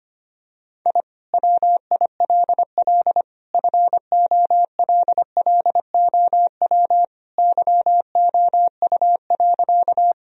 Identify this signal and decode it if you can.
Morse code, heard as a keyed tone, about 25 wpm, 700 Hz. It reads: I WILL FOLLOW YOU.
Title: Sentences from Top 1000 Words - Review Part 1 - 25wpm